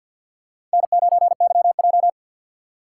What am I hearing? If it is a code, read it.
Morse code, heard as a keyed tone, about 51 wpm, 700 Hz. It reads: N9YJ
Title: USA Call Signs - 6x Character Spacing - 50wpm